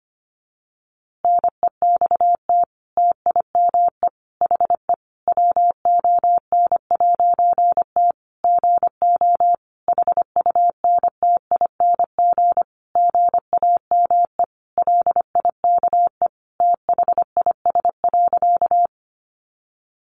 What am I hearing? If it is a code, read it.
Morse code, heard as a keyed tone, about 25 wpm, 700 Hz. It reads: NEXT TIME HE WON'T GO HUNTING GAME LIKE THIS.
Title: A Sound Of Thunder 25wpm